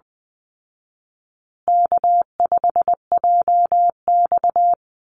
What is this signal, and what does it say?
Morse code, heard as a keyed tone, about 20 wpm, 700 Hz. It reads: K5JX
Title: Call Signs Repeated 2x - 20wpm